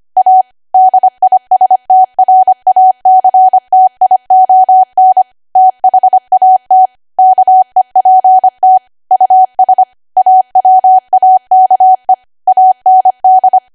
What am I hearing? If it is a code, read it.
Morse code, heard as a keyed tone, about 25 wpm, 750 Hz. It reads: A DISTRACTION THAT KEPT US AWAKE AND
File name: cwa / W1AW-Level3 / 25WPM